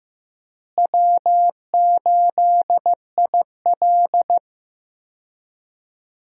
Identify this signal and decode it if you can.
Morse code, heard as a keyed tone, about 15 wpm, 700 Hz. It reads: W8IL
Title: USA Call Signs - 4 Characters - Repeated 2x - 15wpm